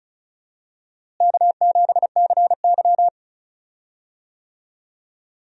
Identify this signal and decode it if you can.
Morse code, heard as a keyed tone, about 36 wpm, 700 Hz. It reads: K7CY
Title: USA Call Signs - 4 Characters - Repeated 3x - 35wpm